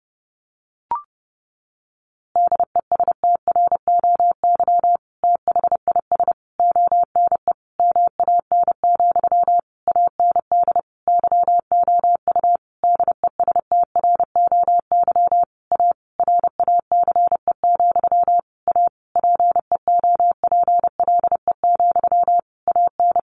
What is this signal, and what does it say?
Morse code, heard as a keyed tone, about 30 wpm, 700 Hz. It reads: DESTROY THIS ONE MAN, AND YOU DESTROY A RACE, A PEOPLE, AN
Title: A Sound Of Thunder 30wpm